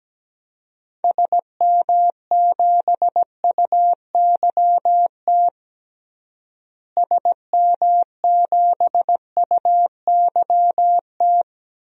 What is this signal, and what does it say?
Morse code, heard as a keyed tone, about 17 wpm, 700 Hz. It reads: SM7UYT SM7UYT
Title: DXCC-100 Call Signs - 6 Characters - Repeated 2x - 17wpm